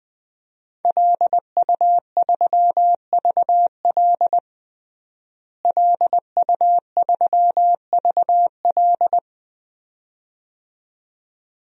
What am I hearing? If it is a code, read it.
Morse code, heard as a keyed tone, about 20 wpm, 700 Hz. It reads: LU3VL LU3VL
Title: DXCC-100 Call Signs - 5 Characters - Repeated 2x - 20wpm